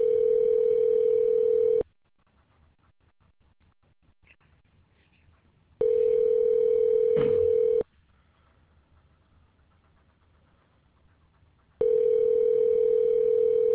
Ringback tone (North America), 440 + 480 Hz.